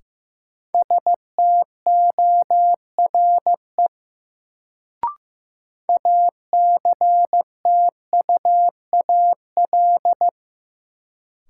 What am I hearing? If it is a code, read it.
Morse code, heard as a keyed tone, about 15 wpm, 700 Hz. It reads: STORE ACTUAL
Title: Sentences from Top 1000 Words - Review Part 5 - New Word Drill - 15wpm